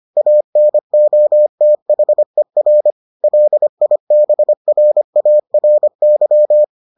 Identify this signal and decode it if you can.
Morse code, heard as a keyed tone, about 25 wpm, 600 Hz. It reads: ANOTHER LIBRARY